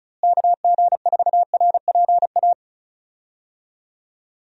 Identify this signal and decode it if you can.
Morse code, heard as a keyed tone, about 36 wpm, 700 Hz. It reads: KG4RPA
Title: USA Call Signs - 6 Characters - Repeated 3x - 35wpm